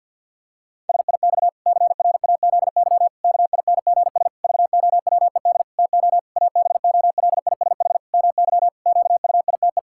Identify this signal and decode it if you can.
Morse code, heard as a keyed tone, about 50 wpm, 700 Hz. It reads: SIX CRAZY KINGS VOWED TO ABOLISH MY QUITE
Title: Pangram Challenge v2 - 50wpm